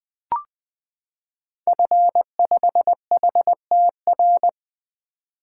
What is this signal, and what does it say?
Morse code, heard as a keyed tone, about 20 wpm, 700 Hz. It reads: F5HTR
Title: 2023 SOTA Activator Call Signs - 20wpm